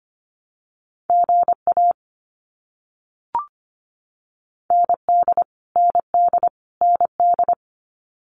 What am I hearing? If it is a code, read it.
Morse code, heard as a keyed tone, about 25 wpm, 700 Hz. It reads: GA ND ND ND
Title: US States Repeated 3x 25wpm